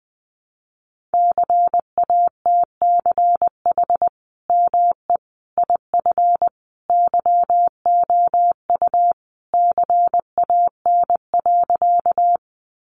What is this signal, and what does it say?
Morse code, heard as a keyed tone, about 20 wpm, 700 Hz. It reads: CATCH ME IF YOU CAN.
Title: Sentences from Top 1500 Words 20wpm